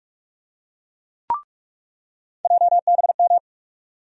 Morse code, heard as a keyed tone, about 47 wpm, 700 Hz. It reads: JBM
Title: USA Call Sign Postfixes - 3 Letters 45wpm